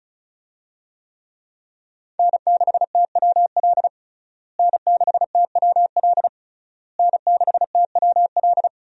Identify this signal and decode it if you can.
Morse code, heard as a keyed tone, about 36 wpm, 700 Hz. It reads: N6TWL N6TWL N6TWL
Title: USA Call Signs - 5 Characters - Repeated 3x - 35wpm